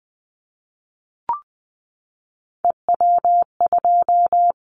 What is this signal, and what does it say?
Morse code, heard as a keyed tone, about 20 wpm, 700 Hz. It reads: EW2
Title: Callsign Prefixes for Top 100 DXCC - 20wpm